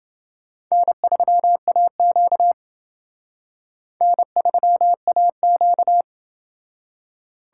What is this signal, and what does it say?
Morse code, heard as a keyed tone, about 28 wpm, 700 Hz. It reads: N3AQ N3AQ